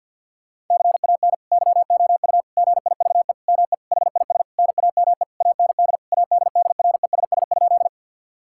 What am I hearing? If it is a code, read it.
Morse code, heard as a keyed tone, about 52 wpm, 700 Hz. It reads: CAN YOU GIVE ME HIS NAME AND ADDRESS?
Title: Sentences from Top 1000 Words - Review Part 6 - 50wpm